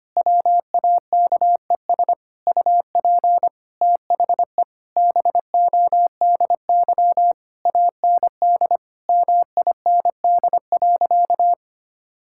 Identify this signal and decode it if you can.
Morse code, heard as a keyed tone, about 25 wpm, 700 Hz. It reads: WAKES UP THE BODY AND MIND.
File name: practice / 25wpm / Difficult Sentences with Hints - 25wpm